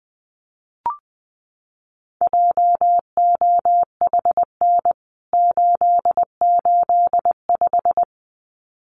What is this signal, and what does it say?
Morse code, heard as a keyed tone, about 20 wpm, 700 Hz. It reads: JOHN 885
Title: CWT Exchange 20wpm